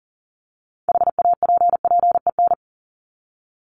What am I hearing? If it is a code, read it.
Morse code, heard as a keyed tone, about 40 wpm, 700 Hz. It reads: HAPPEN
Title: ED Suffix - Top 100 Words with Base Words 40wpm